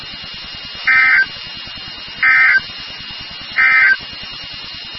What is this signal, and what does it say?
Emergency Alert System (SAME) tones, heard as harsh buzzing bursts. It is heard from 0.8 to 4.0 s.
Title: NOAA Weather Radio WXK36